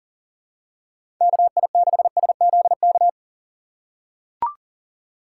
Morse code, heard as a keyed tone, about 40 wpm, 700 Hz. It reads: KI6SZK
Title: USA Call Signs - 6 Characters - 3x Character Spacing - 40wpm